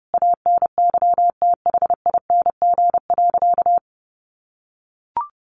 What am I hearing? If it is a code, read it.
Morse code, heard as a keyed tone, about 30 wpm, 700 Hz. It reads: ANYTHING.
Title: Sentences from Top 1000 Words 30wpm